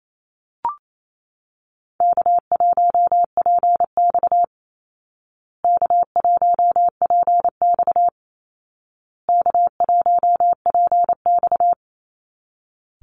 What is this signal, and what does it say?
Morse code, heard as a keyed tone, about 28 wpm, 700 Hz. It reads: K1PX K1PX K1PX
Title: Easy Call Signs Repeated 3x - 28wpm